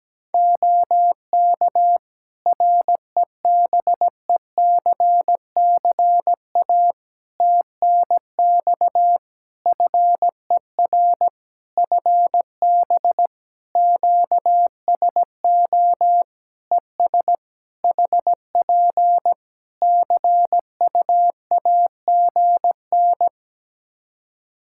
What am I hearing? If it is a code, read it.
Morse code, heard as a keyed tone, about 17 wpm, 700 Hz. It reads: OK REBECCA TNX FER FB QSO ES HP CUAGN
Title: Practice QSO - Exchange 4 - Ending - 17wpm